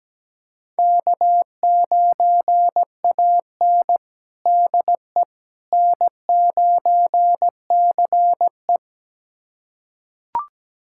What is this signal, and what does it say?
Morse code, heard as a keyed tone, about 17 wpm, 700 Hz. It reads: K9AN DE N9CE
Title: USA Callsign DE Callsign - 4 Characters - 17wpm